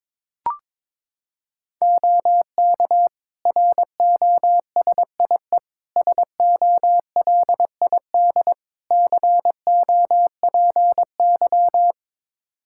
Morse code, heard as a keyed tone, about 22 wpm, 700 Hz. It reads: OK ROSIE SOLID COPY
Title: Practice QSO - Exchange 3 - More Info - 22wpm